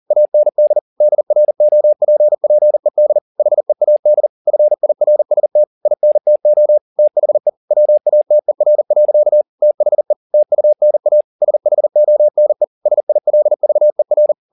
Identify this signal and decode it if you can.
Morse code, heard as a keyed tone, about 40 wpm, 600 Hz. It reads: AND DROPPED HEAD FIRST INTO THE WATER. THE TUNA SHONE SILVER